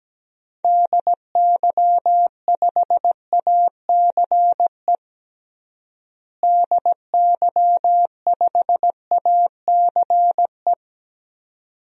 Morse code, heard as a keyed tone, about 17 wpm, 700 Hz. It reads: DY5ACE DY5ACE
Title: DXCC-100 Call Signs - 6 Characters - Repeated 2x - 17wpm